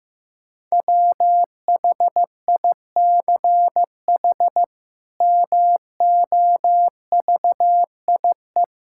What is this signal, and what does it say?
Morse code, heard as a keyed tone, about 15 wpm, 700 Hz. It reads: WHICH MOVIE
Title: Sentences from Top 1000 Words - Review Part 5 - 15wpm